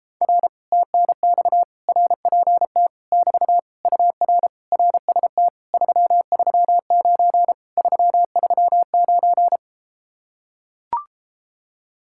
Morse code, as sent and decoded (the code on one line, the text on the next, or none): .-. / - -. -..- / .-. .--. - / -...- / ..- .-. / .-. ... - / ...-- ...-- ----. / ...-- ...-- ----.
R TNX RPT <BT> UR RST 339 339